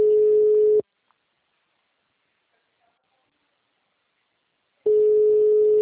Ringback tone (Europe, 425 Hz).
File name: out- -2006